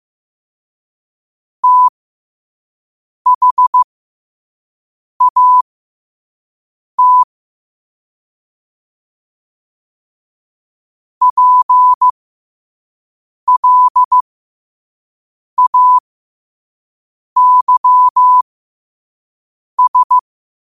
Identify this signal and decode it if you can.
Morse code, heard as a keyed tone, about 15 wpm, spaced as for 5 wpm, 1000 Hz. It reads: THAT PLAYS
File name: qotdfiles / 5wpm